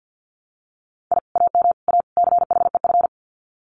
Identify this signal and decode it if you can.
Morse code, heard as a keyed tone, about 48 wpm, 700 Hz. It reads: I AM A CHEF
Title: Common QSO Phrases Encore 50wpm